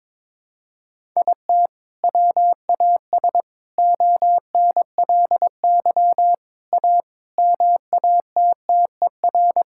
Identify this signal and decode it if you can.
Morse code, heard as a keyed tone, about 22 wpm, 700 Hz. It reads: IT WAS ONLY A MATTER
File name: Sets of 9 Words - 22wpm